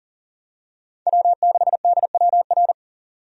Morse code, heard as a keyed tone, about 40 wpm, 700 Hz. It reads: W6DWR
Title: USA Call Signs - 5 Characters - 40wpm